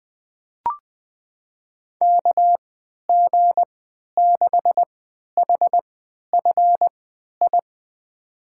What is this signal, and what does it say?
Morse code, heard as a keyed tone, about 20 wpm, spaced as for 11 wpm, 700 Hz. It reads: KG6HFI
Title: USA Call Signs - 6 Characters - 3x Character Spacing - 20wpm